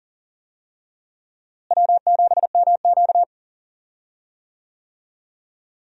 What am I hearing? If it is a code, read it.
Morse code, heard as a keyed tone, about 40 wpm, 700 Hz. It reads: W7MQ
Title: USA Call Signs - 4 Characters - Repeated 3x - 40wpm